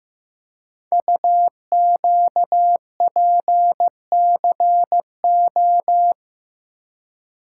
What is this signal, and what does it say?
Morse code, heard as a keyed tone, about 15 wpm, 700 Hz. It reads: UQPCO
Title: Code Groups - 5 Letters - 15wpm